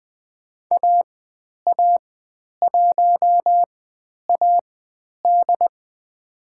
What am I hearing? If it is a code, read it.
Morse code, heard as a keyed tone, about 20 wpm, spaced as for 10 wpm, 700 Hz. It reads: AA1AD
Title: 109 Lesson 18g - Letter m - Call Signs 20wpm